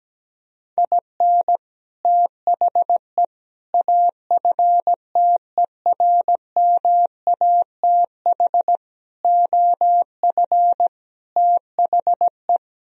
Morse code, heard as a keyed tone, about 17 wpm, 700 Hz. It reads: IN THE AFTERMATH OF THE
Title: Speed-Racing - Sets of 5 Words 17wpm